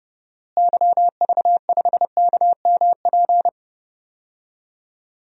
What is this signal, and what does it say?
Morse code, heard as a keyed tone, about 30 wpm, 700 Hz. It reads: YV5KMP
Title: DXCC-100 Call Signs - 6 Characters - Repeated 2x - 30wpm